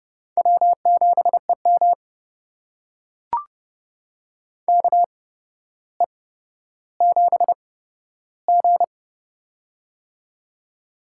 Morse code, heard as a keyed tone, about 30 wpm, spaced as for 13 wpm, 700 Hz. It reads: W7EM KE7G
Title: USA Call Signs - 8x Character Spacing - 30wpm